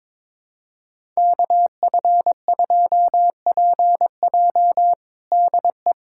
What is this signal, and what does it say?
Morse code, heard as a keyed tone, about 22 wpm, 700 Hz. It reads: KF2PJ DE
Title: USA Callsign DE Callsign - 5 Characters - 22wpm